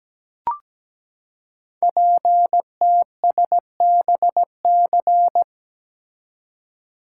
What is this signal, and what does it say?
Morse code, heard as a keyed tone, about 17 wpm, 700 Hz. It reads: PTSBC
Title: Code Groups - 5 Letters - 17wpm